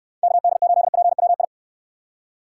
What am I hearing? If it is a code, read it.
Morse code, heard as a keyed tone, about 69 wpm, 700 Hz. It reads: KD8PWI